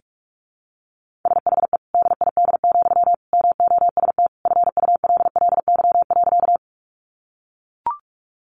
Morse code, heard as a keyed tone, about 44 wpm, 700 Hz. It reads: SHE DID, MOST FULLY.
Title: Sentences from Top 1300 Words 45wpm